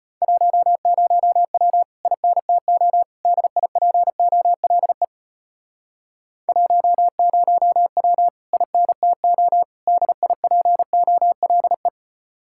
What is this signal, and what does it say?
Morse code, heard as a keyed tone, about 38 wpm, 700 Hz. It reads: 10W INTO DIPOLE 10W INTO DIPOLE